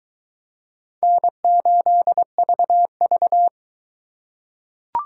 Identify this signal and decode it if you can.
Morse code, heard as a keyed tone, about 21 wpm, 700 Hz. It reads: N8VV